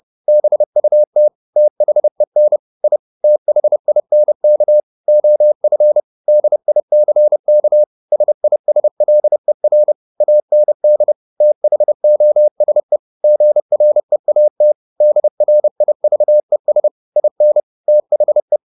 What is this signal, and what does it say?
Morse code, heard as a keyed tone, about 30 wpm, 600 Hz. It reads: BUT THEN I THINK OF DICK SISLER AND THOSE GREAT DRIVES IN THE
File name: the old man and the sea / the old man and the sea 1 30wpm